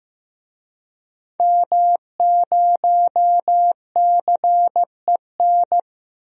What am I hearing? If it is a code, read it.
Morse code, heard as a keyed tone, about 15 wpm, 700 Hz. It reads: M0CEN
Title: DXCC-100 Call Signs - 5 Characters - Repeated 2x - 15wpm